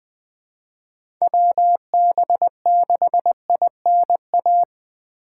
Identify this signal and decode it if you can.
Morse code, heard as a keyed tone, about 20 wpm, 700 Hz. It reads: WB6INA
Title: Speed Racing - USA 2x3 Call Signs 17wpm